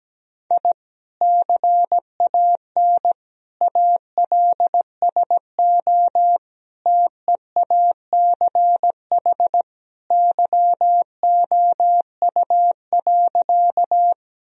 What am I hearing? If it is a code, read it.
Morse code, heard as a keyed tone, about 17 wpm, 700 Hz. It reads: I CAN ALSO TEACH YOU.
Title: Sentences from Top 1000 Words - Review Part 2 - 17wpm